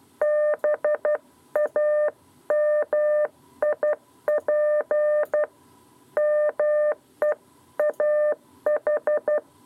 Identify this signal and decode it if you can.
Morse code, heard as a keyed tone, about 12 wpm, 600 Hz. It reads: BAMIP MEAH
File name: 7b lezione cw 2